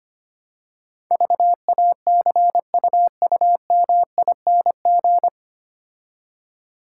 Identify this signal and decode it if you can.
Morse code, heard as a keyed tone, about 25 wpm, 700 Hz. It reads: VACUUMING
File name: Easy Words - Speed-Racing with Hints - 25wpm